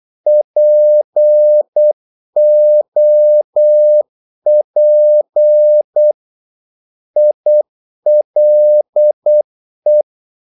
Morse code, heard as a keyed tone, about 8 wpm, 600 Hz. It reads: POP ILE
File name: Korte ord DK 8wpm